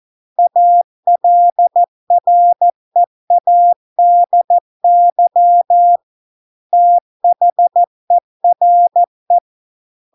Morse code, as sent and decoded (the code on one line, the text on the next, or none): .- .-.. .-. .- -.. -.-- / - .... .-. .
ALREADY THERE